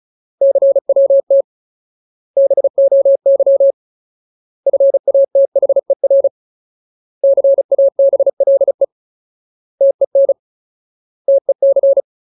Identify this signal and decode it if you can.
Morse code, heard as a keyed tone, about 34 wpm, 550 Hz. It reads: CWT BOY FATHER CABLE TEN TEC
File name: Ultimate Word List / 35WPM Ultimate